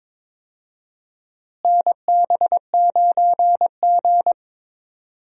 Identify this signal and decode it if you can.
Morse code, heard as a keyed tone, about 22 wpm, 700 Hz. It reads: NB9G
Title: USA Call Signs - 4 Characters - Repeated 3x - 22wpm